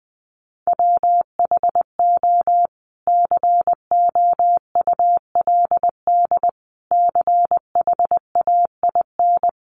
Morse code, heard as a keyed tone, about 20 wpm, 700 Hz. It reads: WHO COULD CHAIN
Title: Sentences from Top 1500 Words 20wpm